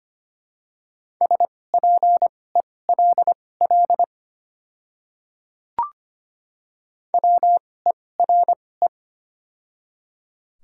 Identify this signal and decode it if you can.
Morse code, heard as a keyed tone, about 25 wpm, spaced as for 18 wpm, 700 Hz. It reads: SPELL WERE
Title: 2x Character Spacing - Top 200 Words - 25wpm